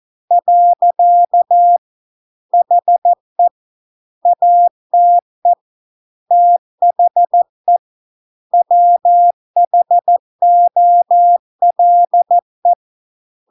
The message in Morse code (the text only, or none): . HE ATE THE WHOLE